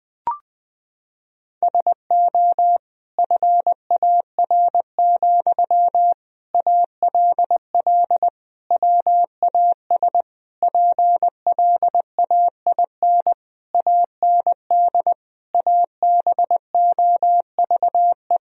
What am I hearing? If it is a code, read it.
Morse code, heard as a keyed tone, about 20 wpm, 700 Hz. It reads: SO FAR, ALL WAS PLAIN AND ABOVE
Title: Sentences from Top 1000 Words - Review Part 4 - 20wpm